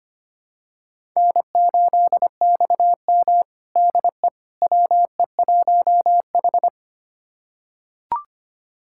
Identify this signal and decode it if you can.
Morse code, heard as a keyed tone, about 25 wpm, 700 Hz. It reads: N8XM DE WE1H
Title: USA Callsign DE Callsign - 4 Characters - 25wpm